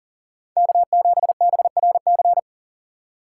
Morse code, heard as a keyed tone, about 40 wpm, 700 Hz. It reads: K7BRC
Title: USA Call Signs - 5 Characters - 2x Character Spacing - 40wpm